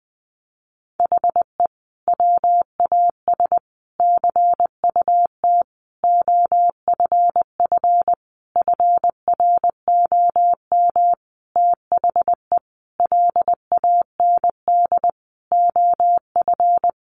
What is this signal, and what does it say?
Morse code, heard as a keyed tone, about 20 wpm, 700 Hz. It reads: HE WAS CUT OFF FROM THE LAND OF
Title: Sentences from Top 1100 Words 20wpm